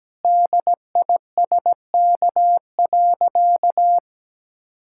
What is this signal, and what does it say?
Morse code, heard as a keyed tone, about 17 wpm, 700 Hz. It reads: DISK.
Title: Sentences from Top 1000 Words - Review Part 6 - 17wpm